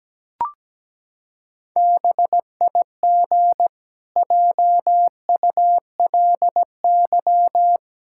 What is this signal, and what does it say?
Morse code, heard as a keyed tone, about 17 wpm, 700 Hz. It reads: BIG JULY
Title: Pangram Challenge v2 - 17wpm